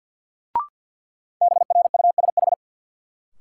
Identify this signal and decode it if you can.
Morse code, heard as a keyed tone, about 52 wpm, 700 Hz. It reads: BRUSH
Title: Sentences from Top 1100 Words - New Word Drill - 50wpm